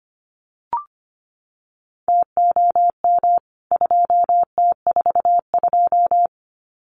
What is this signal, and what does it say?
Morse code, heard as a keyed tone, about 25 wpm, 700 Hz. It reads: TOM 2T42
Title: CWT Exchange 25wpm